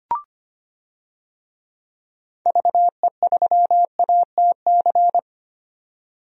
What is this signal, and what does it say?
Morse code, heard as a keyed tone, about 25 wpm, 690 Hz. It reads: VE3ATC
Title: DXCC-100 Call Signs - 6 Characters - 25wpm